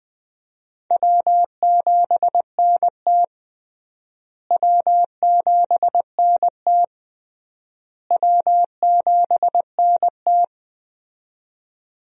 Morse code, heard as a keyed tone, about 20 wpm, 700 Hz. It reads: W7NT W7NT W7NT
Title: USA Call Signs - 4 Characters - Repeated 3x - 20wpm